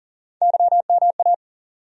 Morse code, heard as a keyed tone, about 40 wpm, 700 Hz. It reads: YMA
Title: USA Call Sign Postfixes - 3 Letters 40wpm